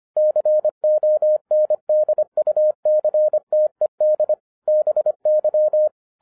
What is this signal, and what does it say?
Morse code, heard as a keyed tone, about 25 wpm, 610 Hz. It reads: CONDUCTED BY